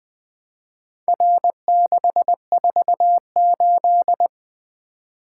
Morse code, heard as a keyed tone, about 20 wpm, 700 Hz. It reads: R648
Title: Four Letter-Number Combinations - 20wpm